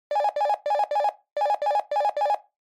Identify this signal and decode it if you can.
There is no code.